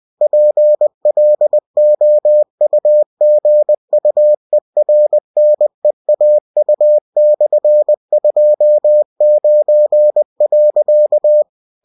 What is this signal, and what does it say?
Morse code, heard as a keyed tone, about 20 wpm, 600 Hz. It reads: PLOUGUERNEAU/29.